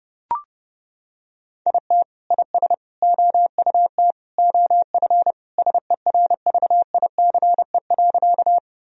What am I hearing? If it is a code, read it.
Morse code, heard as a keyed tone, about 32 wpm, 700 Hz. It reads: IT IS OUT OF SERVICE.
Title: Sentences from Top 1000 Words - Review Part 4 - 30wpm